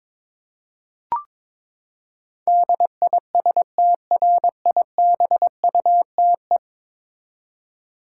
Morse code, heard as a keyed tone, about 21 wpm, 700 Hz. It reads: DISTRIBUTE
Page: morse-code-ninja-practice-22wpm